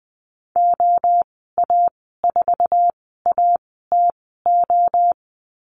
Morse code, heard as a keyed tone, about 20 wpm, spaced as for 14 wpm, 700 Hz. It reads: OA4ATO
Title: 076 Lesson 13g - Letter L - Call Signs 20wpm